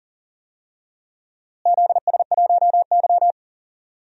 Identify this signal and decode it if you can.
Morse code, heard as a keyed tone, about 40 wpm, 700 Hz. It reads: ZS1Y
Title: DXCC-100 Call Signs - 4 Characters - Repeated 2x - 40wpm